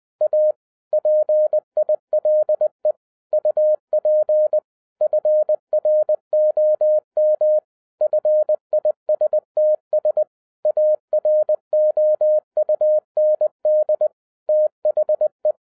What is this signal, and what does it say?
Morse code, heard as a keyed tone, about 20 wpm, 610 Hz. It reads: A PILE UP FROM FISTS AROUND THE
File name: 2012-01-mp3 / n15-INVESTIGATION INTO THE POSSIBILITY OF WINNING PREFIX VERSION 3 AWARD-20wpm